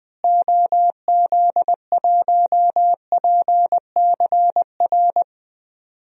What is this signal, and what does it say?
Morse code, heard as a keyed tone, about 20 wpm, 700 Hz. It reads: OZ1PCR
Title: DXCC-100 Call Signs - 6 Characters - Repeated 2x - 20wpm